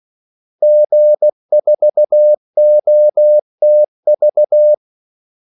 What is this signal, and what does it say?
Morse code, heard as a keyed tone, about 16 wpm, 600 Hz. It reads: G4OTV
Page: Kaldesignaler 16 wpm